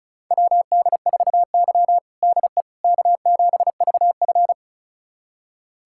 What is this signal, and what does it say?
Morse code, heard as a keyed tone, about 34 wpm, 700 Hz. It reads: WD4Y DE K7VF
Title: USA Callsign DE Callsign - 4 Characters - 35wpm